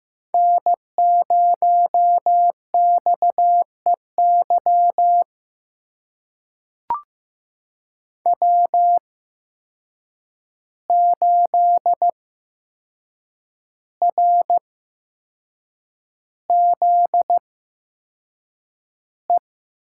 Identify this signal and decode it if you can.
Morse code, heard as a keyed tone, about 15 wpm, spaced as for 6 wpm, 700 Hz. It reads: N0XEY W8RZE
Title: USA Call Signs - 5 Characters - 8x Character Spacing - 15wpm